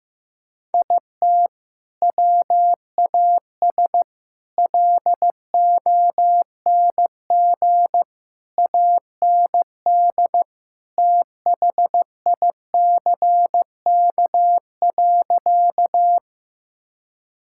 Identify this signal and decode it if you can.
Morse code, heard as a keyed tone, about 16 wpm, 700 Hz. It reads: IT WAS LONG AND THICK.
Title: Sentences from Top 1600 Words 15wpm